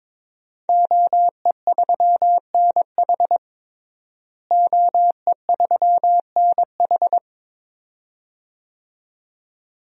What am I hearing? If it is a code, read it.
Morse code, heard as a keyed tone, about 21 wpm, 700 Hz. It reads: OE3NH OE3NH
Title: DXCC-100 Call Signs - 5 Characters - Repeated 2x - 22wpm